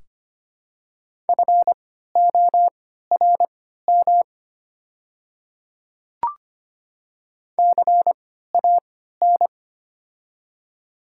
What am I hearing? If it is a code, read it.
Morse code, heard as a keyed tone, about 25 wpm, spaced as for 14 wpm, 700 Hz. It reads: FORM CAN
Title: 3x Character Spacing - Top 200 Words - 25wpm